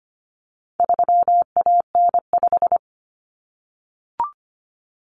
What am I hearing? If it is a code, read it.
Morse code, heard as a keyed tone, about 25 wpm, 700 Hz. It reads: 3AN5
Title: Common Cut Numbers - 4-Digit 25wpm